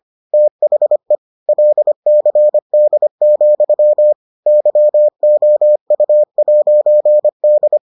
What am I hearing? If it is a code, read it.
Morse code, heard as a keyed tone, about 25 wpm, 600 Hz. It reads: THE LCD, YOU'D